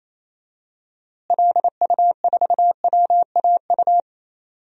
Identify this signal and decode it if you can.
Morse code, heard as a keyed tone, about 28 wpm, 700 Hz. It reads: LU4WAU
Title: DXCC-100 Call Signs - 6 Characters - Repeated 3x - 28wpm